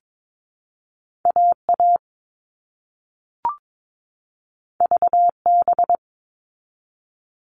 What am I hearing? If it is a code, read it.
Morse code, heard as a keyed tone, about 22 wpm, 700 Hz. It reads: AA VB
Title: All Cut Numbers - 2-Digit 22wpm